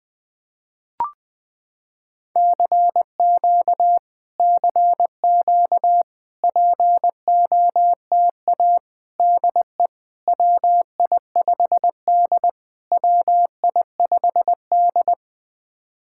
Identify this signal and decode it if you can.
Morse code, heard as a keyed tone, about 20 wpm, 700 Hz. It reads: CQ CQ POTA DE WI5D WI5D
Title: POTA QSO Practice - 20wpm